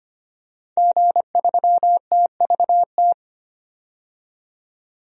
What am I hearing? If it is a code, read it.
Morse code, heard as a keyed tone, about 25 wpm, 700 Hz. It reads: G3TVT
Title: DXCC-100 Call Signs - 5 Characters - 25wpm